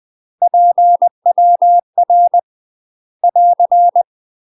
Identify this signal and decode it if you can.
Morse code, heard as a keyed tone, about 20 wpm, 700 Hz. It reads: PWR <AR>
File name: UK MASTER QSOs TRAINING / 20WPM / UK MASTER QSOs 0014